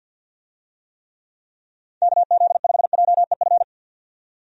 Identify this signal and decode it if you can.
Morse code, heard as a keyed tone, about 52 wpm, 700 Hz. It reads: KZ5JEF
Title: USA Call Signs - 6 Characters - 50wpm